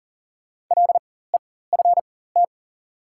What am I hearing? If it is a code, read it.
Morse code, heard as a keyed tone, about 37 wpm, spaced as for 18 wpm, 700 Hz. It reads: LEFT
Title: 4x Character Spacing - Top 300 Words - 40wpm